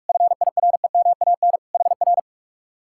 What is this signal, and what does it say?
Morse code, heard as a keyed tone, about 49 wpm, 700 Hz. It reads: FIREMAN HR